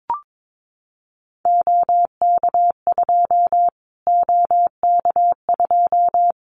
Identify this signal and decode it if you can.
Morse code, heard as a keyed tone, about 22 wpm, 700 Hz. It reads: OK2 OK2
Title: Callsign Prefixes for Top 25 DXCC - Repeated 3x - 22wpm